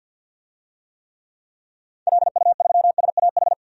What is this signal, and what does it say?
Morse code, heard as a keyed tone, about 47 wpm, 700 Hz. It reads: LU3SAH
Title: DXCC-100 Call Signs - 6 Characters - 50wpm